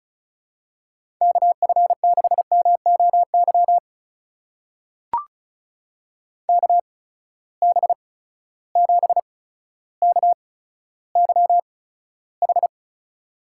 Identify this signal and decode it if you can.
Morse code, heard as a keyed tone, about 36 wpm, spaced as for 15 wpm, 700 Hz. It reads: KF6MOY KB7KYH
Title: USA Call Signs - 6 Characters - 8x Character Spacing - 35wpm